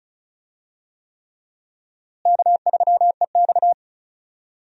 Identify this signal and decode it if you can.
Morse code, heard as a keyed tone, about 36 wpm, 700 Hz. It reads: K3EX
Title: USA Call Signs - 4 Characters - Repeated 3x - 35wpm